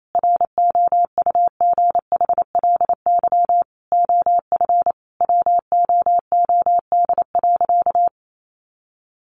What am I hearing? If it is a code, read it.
Morse code, heard as a keyed tone, about 28 wpm, 700 Hz. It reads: ROUGHLY OF WOOD.
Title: Sentences from Top 1500 Words 28wpm